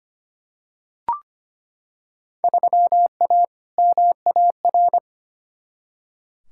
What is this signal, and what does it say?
Morse code, heard as a keyed tone, about 25 wpm, 700 Hz. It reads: 3A MAR
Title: Field Day Exchanges Encore - 25wpm